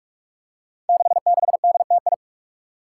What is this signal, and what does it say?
Morse code, heard as a keyed tone, about 44 wpm, 700 Hz. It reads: 66DTI
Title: Five Challenging Letter-Number Combinations 45wpm